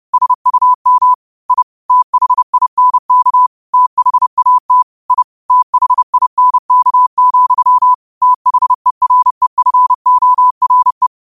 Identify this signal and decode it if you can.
Morse code, heard as a keyed tone, about 30 wpm, 1000 Hz. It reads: SUM I THINK THAT I THINK, THEREFORE
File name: 2026-02-23-30wpm